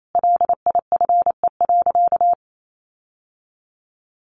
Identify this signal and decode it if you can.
Morse code, heard as a keyed tone, about 28 wpm, 700 Hz. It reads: LIFE.
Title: Sentences from Top 1200 Words 28wpm